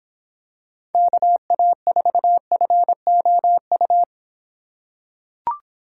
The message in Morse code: KA4FOU

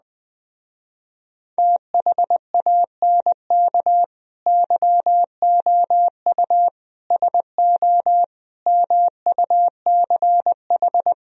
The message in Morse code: THANK YOU SO MUCH

